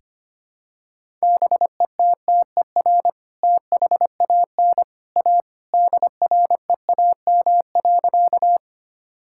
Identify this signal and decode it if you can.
Morse code, heard as a keyed tone, about 25 wpm, 700 Hz. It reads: BETTER THAN A DREAM.
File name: practice / 25wpm / Sentences with 2-Word Phrases - 25wpm